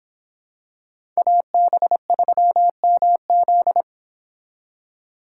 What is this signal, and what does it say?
Morse code, heard as a keyed tone, about 26 wpm, 700 Hz. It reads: AB3MZ